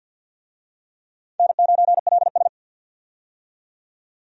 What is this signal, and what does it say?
Morse code, heard as a keyed tone, about 46 wpm, 700 Hz. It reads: N9LS
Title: USA Call Signs - 4 Characters - 50wpm